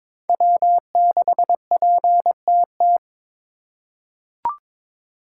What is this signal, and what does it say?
Morse code, heard as a keyed tone, about 22 wpm, 700 Hz. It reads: W6PTT
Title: USA Call Signs - 5 Characters - 2x Character Spacing - 22wpm